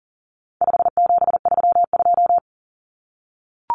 Morse code, heard as a keyed tone, about 40 wpm, 700 Hz. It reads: 5732
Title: Numbers - 4-Digit 40wpm